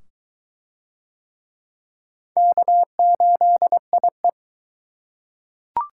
Morse code, heard as a keyed tone, about 20 wpm, 700 Hz. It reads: K8IE